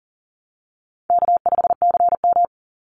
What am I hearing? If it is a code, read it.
Morse code, heard as a keyed tone, about 40 wpm, 700 Hz. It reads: K5CM
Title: Easy Call Signs Repeated 2x - 40wpm